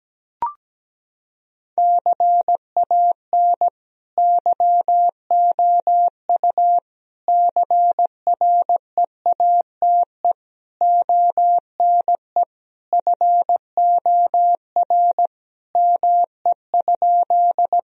Morse code, heard as a keyed tone, about 17 wpm, 700 Hz. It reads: CAN YOU CREATE ONE FOR ME?
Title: Sentences from Top 1000 Words - Review Part 3 - 17wpm